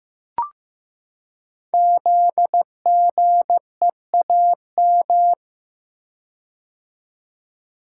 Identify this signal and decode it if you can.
Morse code, heard as a keyed tone, about 15 wpm, 700 Hz. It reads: ZGEAM
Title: Code Groups - 5 Letters - 15wpm